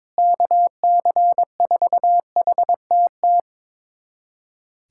Morse code, heard as a keyed tone, about 22 wpm, 700 Hz. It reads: KC4HTT
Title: USA Callsign DE Callsign 22wpm